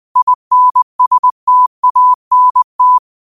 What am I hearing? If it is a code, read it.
Morse code, heard as a keyed tone, about 20 wpm, 1000 Hz. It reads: INSTANT
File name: qotdfiles / 20wpm